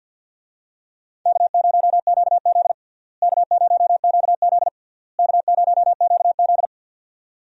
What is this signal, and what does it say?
Morse code, heard as a keyed tone, about 52 wpm, 700 Hz. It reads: K0QZ K0QZ K0QZ
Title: USA Call Signs - 4 Characters - Repeated 3x - 50wpm